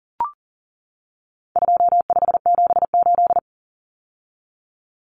Morse code, heard as a keyed tone, about 40 wpm, 700 Hz. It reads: 2578
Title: Common Cut Numbers - 4-Digit 40wpm